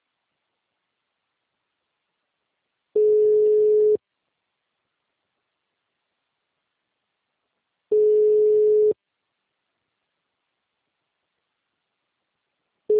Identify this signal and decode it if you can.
Ringback tone (Europe, 425 Hz).